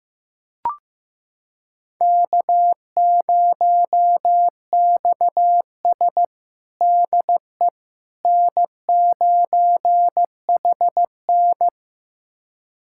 Morse code, heard as a keyed tone, about 15 wpm, 700 Hz. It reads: K0XS DE N9HN
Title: USA Callsign DE Callsign - 4 Characters - 15wpm